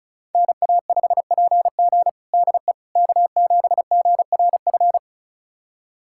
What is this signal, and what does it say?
Morse code, heard as a keyed tone, about 36 wpm, 700 Hz. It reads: NA5PG DE K7GRF
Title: USA Callsign DE Callsign - 5 Characters - 35wpm